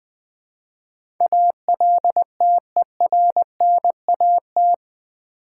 Morse code, heard as a keyed tone, about 20 wpm, 700 Hz. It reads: ALTERNAT
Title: Build A Word - 4 Letters At A Time - 11-Letter Words - 20wpm